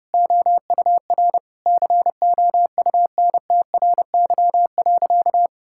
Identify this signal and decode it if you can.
Morse code, heard as a keyed tone, about 30 wpm, 700 Hz. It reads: OUR COUNTRY.
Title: Sentences from Top 1000 Words - Review Part 6 - 30wpm